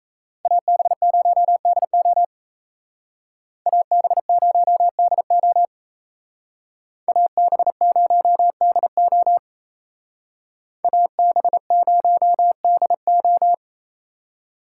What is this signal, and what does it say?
Morse code, heard as a keyed tone, about 33 wpm, 700 Hz. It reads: AB0DO AB0DO AB0DO AB0DO
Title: Speed Racing - USA 2x2 Call Signs 28wpm